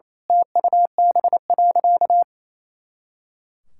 Morse code, heard as a keyed tone, about 27 wpm, 700 Hz. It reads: TUB.
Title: Pangram Challenge v2 - 28wpm